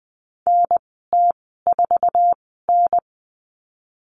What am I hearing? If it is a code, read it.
Morse code, heard as a keyed tone, about 20 wpm, spaced as for 15 wpm, 700 Hz. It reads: NT4N
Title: 048 Lesson 9g - Number 4 - Call Signs 20wpm